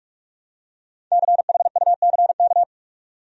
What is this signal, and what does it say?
Morse code, heard as a keyed tone, about 46 wpm, 700 Hz. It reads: CHUCK
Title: USA Common Names - Max 5 Letters - 45wpm